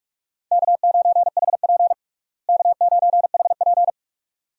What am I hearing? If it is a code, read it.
Morse code, heard as a keyed tone, about 46 wpm, 700 Hz. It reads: K9HP K9HP
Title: USA Call Signs - 4 Characters - Repeated 2x - 45wpm